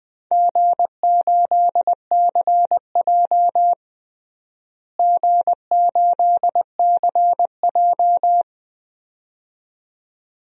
Morse code, heard as a keyed tone, about 21 wpm, 700 Hz. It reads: G8CJ G8CJ
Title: DXCC-100 Call Signs - 4 Characters - Repeated 2x - 20wpm